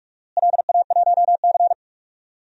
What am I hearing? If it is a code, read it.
Morse code, heard as a keyed tone, about 46 wpm, 700 Hz. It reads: LA1C
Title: DXCC-100 Call Signs - 4 Characters - Repeated 3x - 45wpm